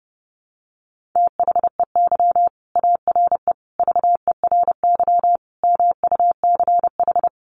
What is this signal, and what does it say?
Morse code, heard as a keyed tone, about 30 wpm, 700 Hz. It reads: THEY ARE VERY MUCH
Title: Sentences from Top 1500 Words 30wpm